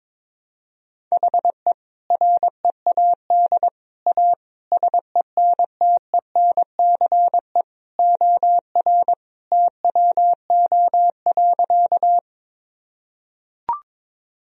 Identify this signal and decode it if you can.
Morse code, heard as a keyed tone, about 22 wpm, 700 Hz. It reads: HE READ A SENTENCE OR TWO.
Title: Sentences from Top 1000 Words - Review Part 1 - 22wpm